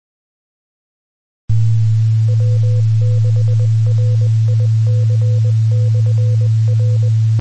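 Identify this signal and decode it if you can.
Morse code, heard as a keyed tone, about 20 wpm, 500 Hz, with noise. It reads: W6RIC/R
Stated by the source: Recording of the 146.850 MHz two-metre repeater.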